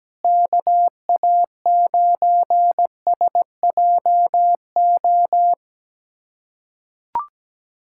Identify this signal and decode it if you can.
Morse code, heard as a keyed tone, about 17 wpm, 700 Hz. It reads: KA9SJO
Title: USA Callsign DE Callsign - 6 Characters - 17wpm